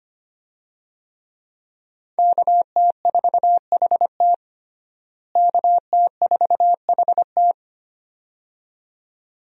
Morse code, heard as a keyed tone, about 25 wpm, 700 Hz. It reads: KT4HT KT4HT
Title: USA Call Signs - 5 Characters - Repeated 2x - 25wpm